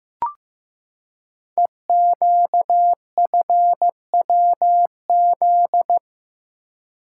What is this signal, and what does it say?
Morse code, heard as a keyed tone, about 15 wpm, 700 Hz. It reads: EQFWZ
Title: Code Groups - 5 Letter:Numbers - 15wpm